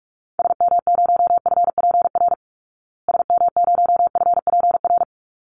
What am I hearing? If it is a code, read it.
Morse code, heard as a keyed tone, about 44 wpm, 700 Hz. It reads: SM0FPR SM0FPR
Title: Call Signs Repeated 3x - 45wpm